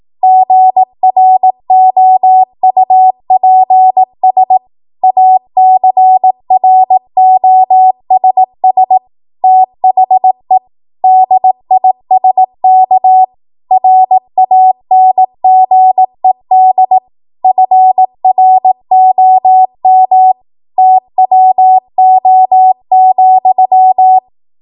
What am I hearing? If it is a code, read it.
Morse code, heard as a keyed tone, about 18 wpm, 750 Hz. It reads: GROUPS ACROSS THE DISK RANGED FROM TWO,
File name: Morse / 18WPM